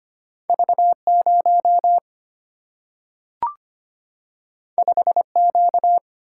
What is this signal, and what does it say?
Morse code, heard as a keyed tone, about 25 wpm, 700 Hz. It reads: V0 5Q
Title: Single-Letter Single-Number 25wpm